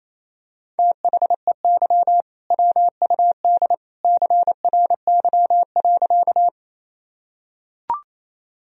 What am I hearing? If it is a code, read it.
Morse code, heard as a keyed tone, about 28 wpm, 700 Hz. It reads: THEY WUD CRY.
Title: Sentences from Top 500 Words and Abbreviations 28wpm